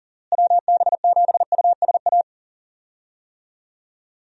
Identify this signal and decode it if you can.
Morse code, heard as a keyed tone, about 39 wpm, 700 Hz. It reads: WB7USA
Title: USA Call Signs - 6 Characters - Repeated 3x - 40wpm